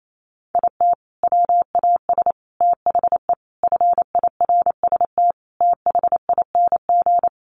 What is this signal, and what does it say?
Morse code, heard as a keyed tone, about 28 wpm, 700 Hz. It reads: IT WAS THE FIRST THING
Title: Sentences from Top 300 Words 28wpm